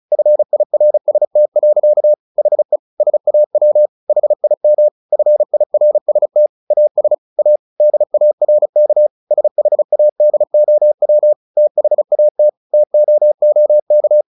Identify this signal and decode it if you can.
Morse code, heard as a keyed tone, about 35 wpm, 600 Hz. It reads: FIRST. HE SAW HIM FIRST AS A DARK SHADOW THAT TOOK
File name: the old man and the sea 2 35wpm